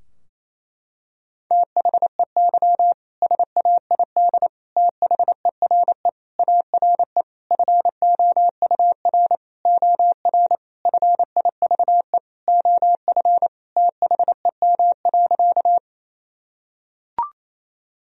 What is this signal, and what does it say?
Morse code, heard as a keyed tone, about 28 wpm, 700 Hz. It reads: THEY SAID THERE ARE FOUR OR FIVE OF THEM.
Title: Sentences from Top 1000 Words - Review Part 2 - 28wpm